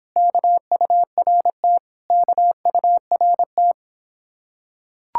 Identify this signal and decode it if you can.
Morse code, heard as a keyed tone, about 26 wpm, 700 Hz. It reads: KURT KURT